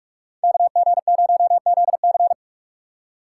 Morse code, heard as a keyed tone, about 46 wpm, 700 Hz. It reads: KG0ZC
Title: USA Call Signs - 5 Characters - 4x Character Spacing - 45wpm